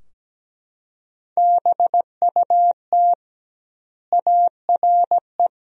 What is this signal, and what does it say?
Morse code, heard as a keyed tone, about 17 wpm, 700 Hz. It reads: BUT ARE
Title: 2x Word-Spacing - Sentences from Top 300 Words - 17wpm